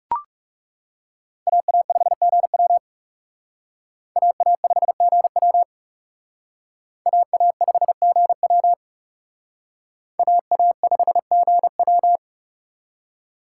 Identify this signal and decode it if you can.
Morse code, heard as a keyed tone, about 37 wpm, 700 Hz. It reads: AA5GW AA5GW AA5GW AA5GW
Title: Speed Racing - USA Call Signs 30wpm